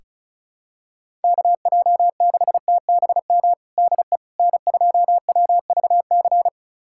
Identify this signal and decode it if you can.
Morse code, heard as a keyed tone, about 34 wpm, 700 Hz. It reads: KJ6TBM DE N2WVC
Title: USA Callsign DE Callsign 35wpm